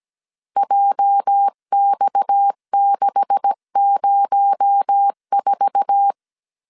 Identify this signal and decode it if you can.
Morse code, heard as a keyed tone, about 17 wpm, 800 Hz. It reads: JX604
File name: 1500C S B 16wpm